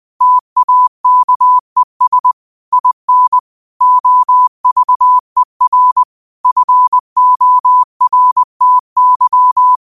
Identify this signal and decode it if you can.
Morse code, heard as a keyed tone, about 20 wpm, 1000 Hz. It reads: TAKES IN OVER FORTY
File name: qotdfiles / 20wpm